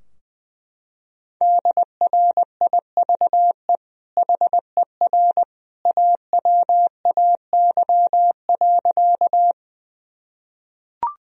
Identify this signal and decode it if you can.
Morse code, heard as a keyed tone, about 19 wpm, 700 Hz. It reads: DRIVE HER AWAY.
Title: Sentences from Top 1000 Words - Review Part 2 - 20wpm